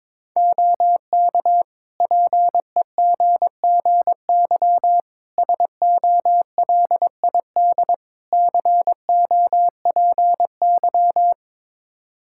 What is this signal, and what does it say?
Morse code, heard as a keyed tone, about 22 wpm, 700 Hz. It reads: OK PEGGY SOLID COPY
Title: Practice QSO - Exchange 3 - More Info - 22wpm